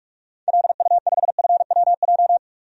Morse code, heard as a keyed tone, about 42 wpm, 700 Hz. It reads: LU5FWJ
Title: DXCC-100 Call Signs - 6 Characters - 45wpm